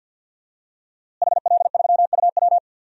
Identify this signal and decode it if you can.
Morse code, heard as a keyed tone, about 52 wpm, 700 Hz. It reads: HL3UW
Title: DXCC-100 Call Signs - 5 Characters - Repeated 2x - 50wpm